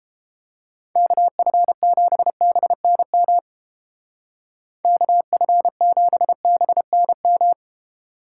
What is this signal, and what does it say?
Morse code, heard as a keyed tone, about 29 wpm, 700 Hz. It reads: KF7BNM KF7BNM